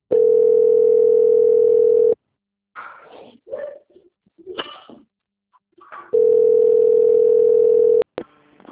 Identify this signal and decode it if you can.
Ringback tone (North America), 440 + 480 Hz.